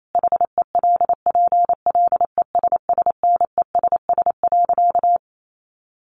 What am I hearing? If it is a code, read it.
Morse code, heard as a keyed tone, about 28 wpm, 700 Hz. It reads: HELPLESSNESS.
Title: A Sound Of Thunder 28wpm